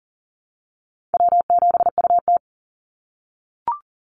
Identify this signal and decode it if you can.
Morse code, heard as a keyed tone, about 40 wpm, 700 Hz. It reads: W7UT
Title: Easy Call Signs - 40wpm